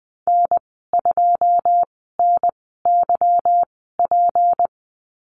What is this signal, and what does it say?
Morse code, heard as a keyed tone, about 20 wpm, spaced as for 15 wpm, 700 Hz. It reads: N2NYP
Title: 173 Lesson 28g - Number 9 - Call Signs 20wpm